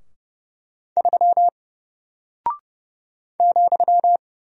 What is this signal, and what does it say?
Morse code, heard as a keyed tone, about 30 wpm, 700 Hz. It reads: 3 ,
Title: Single Character (Letter:Number:Punctuation:Prosign) 30wpm